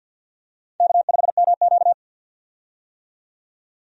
Morse code, heard as a keyed tone, about 52 wpm, 700 Hz. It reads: K5MQ
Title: USA Call Signs - 4 Characters - 50wpm